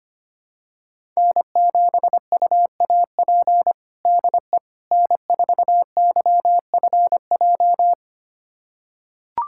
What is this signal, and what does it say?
Morse code, heard as a keyed tone, about 25 wpm, 700 Hz. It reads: N7UAP DE N4YFJ
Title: USA Callsign DE Callsign 25wpm